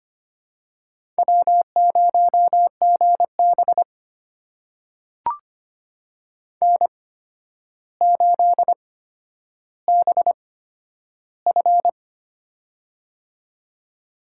Morse code, heard as a keyed tone, about 25 wpm, spaced as for 11 wpm, 700 Hz. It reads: W0GB N8BF
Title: USA Call Signs - 4 Characters - 8x Character Spacing - 25wpm